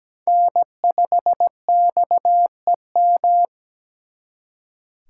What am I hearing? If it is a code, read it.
Morse code, heard as a keyed tone, about 20 wpm, 700 Hz. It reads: N5XEM
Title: Speed Racing - USA 1x3 Call Signs 17wpm